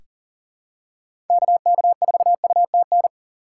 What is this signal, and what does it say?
Morse code, heard as a keyed tone, about 40 wpm, 700 Hz. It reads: KK4UTN
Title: USA Call Signs - 6 Characters - 2x Character Spacing - 40wpm